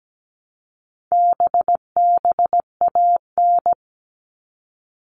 Morse code, heard as a keyed tone, about 17 wpm, 700 Hz. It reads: BBAN
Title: All Cut Numbers - 4-Digit 17wpm